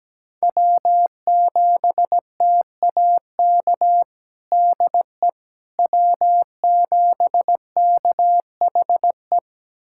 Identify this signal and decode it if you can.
Morse code, heard as a keyed tone, about 17 wpm, 700 Hz. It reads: W7TAK DE W7KHE
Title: USA Callsign DE Callsign - 5 Characters - 17wpm